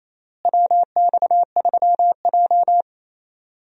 Morse code, heard as a keyed tone, about 28 wpm, 700 Hz. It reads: WX3J
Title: USA Call Signs - 4 Characters - Repeated 2x - 28wpm